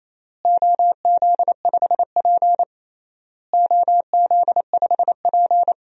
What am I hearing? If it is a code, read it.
Morse code, heard as a keyed tone, about 28 wpm, 700 Hz. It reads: OZ5P OZ5P
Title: DXCC-100 Call Signs - 4 Characters - Repeated 3x - 28wpm